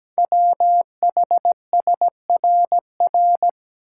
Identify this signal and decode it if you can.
Morse code, heard as a keyed tone, about 17 wpm, 700 Hz. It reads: WHSRR
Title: Code Groups - 5 Letters - 17wpm